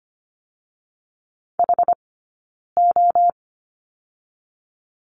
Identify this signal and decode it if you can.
Morse code, heard as a keyed tone, about 25 wpm, 700 Hz. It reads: H O
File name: course / 25:10wpm / 092 Lesson 16d - Letter c - Two Characters 25wpm